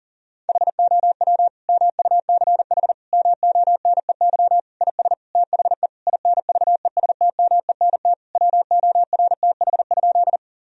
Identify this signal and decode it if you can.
Morse code, heard as a keyed tone, about 40 wpm, 700 Hz. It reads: HOW MUCH MONEY IS THE INVESTMENT WORTH?
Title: Sentences from Top 1000 Words - Review Part 4 - 40wpm